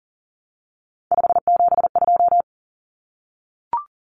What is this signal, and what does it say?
Morse code, heard as a keyed tone, about 40 wpm, 700 Hz. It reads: 572
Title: Common Cut Numbers - 3-Digit 40wpm